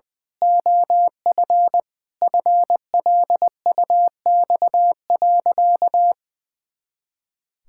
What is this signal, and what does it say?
Morse code, heard as a keyed tone, about 20 wpm, 700 Hz. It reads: OF FLUX.
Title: Pangram Challenge v2 - 20wpm